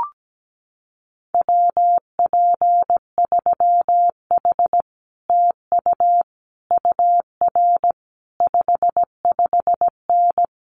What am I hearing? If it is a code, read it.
Morse code, heard as a keyed tone, about 17 wpm, 700 Hz. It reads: WP3H TU UR 55N
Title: Common QSO Phrases Encore 17wpm